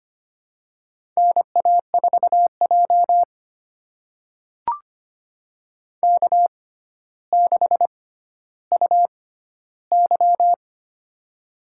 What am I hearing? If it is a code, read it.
Morse code, heard as a keyed tone, about 25 wpm, spaced as for 13 wpm, 700 Hz. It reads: NA4J K6UY
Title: USA Call Signs - 4 Characters - 6x Character Spacing - 25wpm